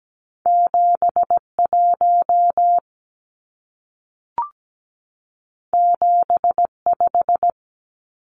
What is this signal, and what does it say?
Morse code, heard as a keyed tone, about 17 wpm, 700 Hz. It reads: 71 75
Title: Numbers - 2-Digit 17wpm